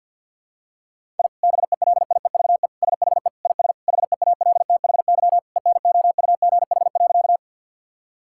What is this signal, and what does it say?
Morse code, heard as a keyed tone, about 47 wpm, 700 Hz. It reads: I BELIEVE SHE IS HEALTHY ENOUGH.
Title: Sentences from Top 1000 Words - Review Part 5 - 50wpm